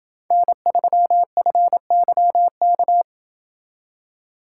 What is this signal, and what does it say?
Morse code, heard as a keyed tone, about 28 wpm, 700 Hz. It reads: N3FYK